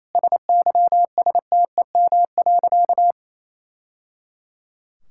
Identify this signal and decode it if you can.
Morse code, heard as a keyed tone, about 28 wpm, 700 Hz. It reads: SYSTEM.
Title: Sentences from Top 1000 Words - Review Part 6 - 28wpm